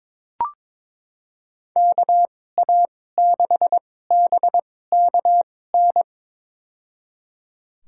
Morse code, heard as a keyed tone, about 22 wpm, spaced as for 16 wpm, 700 Hz. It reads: KA6BKN
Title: USA Call Signs - 2x Character Spacing - 22wpm